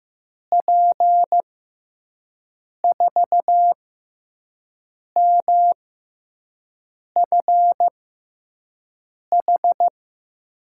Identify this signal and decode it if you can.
Morse code, heard as a keyed tone, about 15 wpm, spaced as for 5 wpm, 700 Hz. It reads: P4MFH
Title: USA Call Signs - 6 Characters - 6x Character Spacing - 15wpm